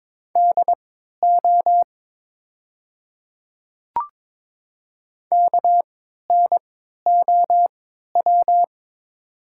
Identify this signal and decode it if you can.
Morse code, heard as a keyed tone, about 22 wpm, spaced as for 13 wpm, 700 Hz. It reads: DO KNOW
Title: 3x Character Spacing - Top 100 Words - 22wpm